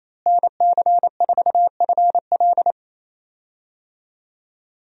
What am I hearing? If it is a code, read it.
Morse code, heard as a keyed tone, about 28 wpm, 700 Hz. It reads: NC4FL
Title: USA Call Signs - 5 Characters - Repeated 2x - 28wpm